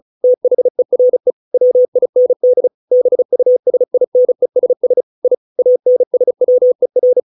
Morse code, heard as a keyed tone, about 34 wpm, 500 Hz. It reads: THERE WIND BUSINESS I ANSWER